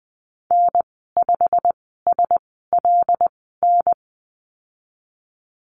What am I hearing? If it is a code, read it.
Morse code, heard as a keyed tone, about 20 wpm, spaced as for 15 wpm, 700 Hz. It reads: N5SLN
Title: 154 Lesson 25g - Letter p - Call Signs 20wpm